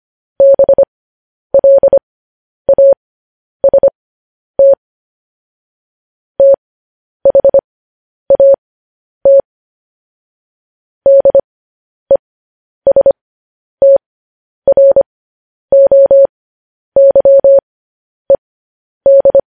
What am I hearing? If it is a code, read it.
Morse code, heard as a keyed tone, about 25 wpm, spaced as for 10 wpm, 550 Hz. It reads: BLAST THAT DESTROYED
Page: News Headlines in Morse Code at 10 WPM